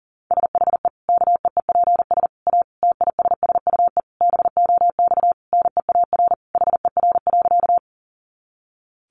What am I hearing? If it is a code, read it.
Morse code, heard as a keyed tone, about 40 wpm, 700 Hz. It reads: SHE KEEPS A TISSUE BOX NEAR HER.
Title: Easy Phrases 40wpm